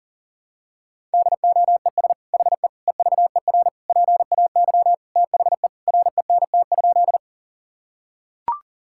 Morse code, heard as a keyed tone, about 37 wpm, 700 Hz. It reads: DOES HE EVER PAY THE RENT?
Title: Sentences from Top 1000 Words - Review Part 5 - 40wpm